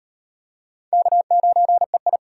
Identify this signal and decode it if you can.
Morse code, heard as a keyed tone, about 38 wpm, 700 Hz. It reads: K9EI